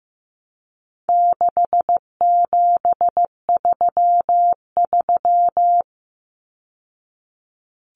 Morse code, heard as a keyed tone, about 15 wpm, 700 Hz. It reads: 6733
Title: Numbers - 4-Digit 15wpm